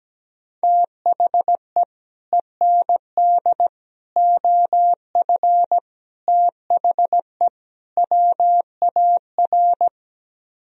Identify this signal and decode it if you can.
Morse code, heard as a keyed tone, about 17 wpm, 700 Hz. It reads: THE END OF THE WAR
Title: Speed-Racing - Sets of 5 Words 17wpm